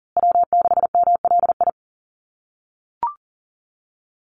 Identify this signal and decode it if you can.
Morse code, heard as a keyed tone, about 40 wpm, 700 Hz. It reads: W6MLI
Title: USA Call Signs Repeated 2x 40wpm